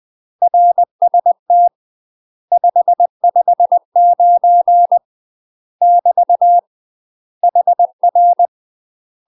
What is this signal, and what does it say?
Morse code, heard as a keyed tone, about 20 wpm, 700 Hz. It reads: RST 559 <BT> HR
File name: UK MASTER QSOs TRAINING / 20WPM / UK MASTER QSOs 0022